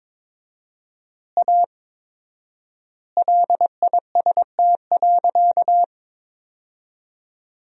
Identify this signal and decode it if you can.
Morse code, heard as a keyed tone, about 23 wpm, 700 Hz. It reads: A LIST.
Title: 4x Word-Spacing - Sentences from Top 400 Words - 22wpm